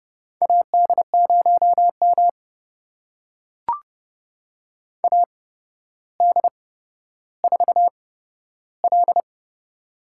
Morse code, heard as a keyed tone, about 30 wpm, spaced as for 13 wpm, 700 Hz. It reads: AD0M AD4L
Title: USA Call Signs - 4 Characters - 8x Character Spacing - 30wpm